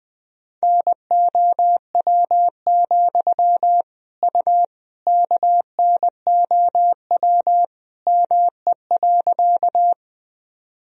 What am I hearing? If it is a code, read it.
Morse code, heard as a keyed tone, about 20 wpm, 700 Hz. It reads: NOW, U KNOW ME.
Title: Sentences from Top 200 Words and Abbreviations 20wpm